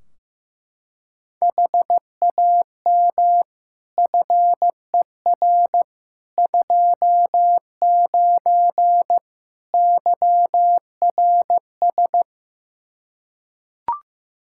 Morse code, heard as a keyed tone, about 15 wpm, 700 Hz. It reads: HAM FER 29 YRS
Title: Practice QSO - Exchange 3 - More Info - 15wpm